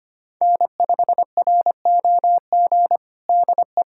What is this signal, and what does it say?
Morse code, heard as a keyed tone, about 25 wpm, 700 Hz. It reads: N5ROG DE
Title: USA Callsign DE Callsign - 5 Characters - 25wpm